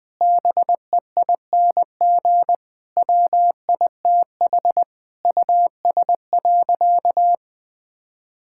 Morse code, heard as a keyed tone, about 20 wpm, 700 Hz. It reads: BEING WITH US.
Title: Sentences with 5-Word Phrases - 20wpm